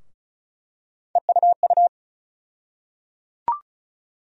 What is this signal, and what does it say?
Morse code, heard as a keyed tone, about 34 wpm, 700 Hz. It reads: EUU
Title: USA Call Sign Postfixes - 3 Letters 35wpm